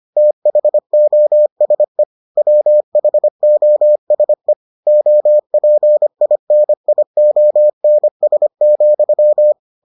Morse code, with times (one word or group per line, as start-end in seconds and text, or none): 0.2-2.0 THOSE
2.4-4.5 WHOSE
4.9-9.5 OPINIONS,